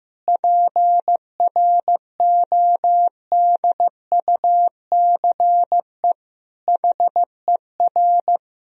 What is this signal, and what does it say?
Morse code, heard as a keyed tone, about 16 wpm, 700 Hz. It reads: PRODUCE HER
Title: Sentences from Top 1000 Words - Review Part 3 - 15wpm